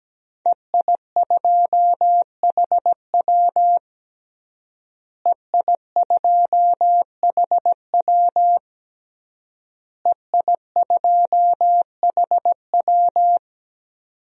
Morse code, heard as a keyed tone, about 17 wpm, 700 Hz. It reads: EI2HW EI2HW EI2HW
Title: DXCC-100 Call Signs - 5 Characters - Repeated 3x - 17wpm